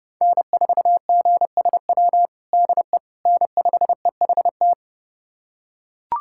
Morse code, heard as a keyed tone, about 32 wpm, 700 Hz. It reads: N4GSW DE N5EHT
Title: USA Callsign DE Callsign - 5 Characters - 30wpm